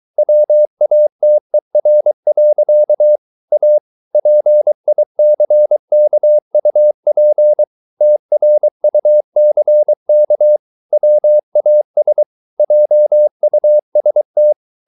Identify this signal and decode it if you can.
Morse code, heard as a keyed tone, about 23 wpm, 600 Hz. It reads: WATER. A PICKUP TRUCK WAS JUST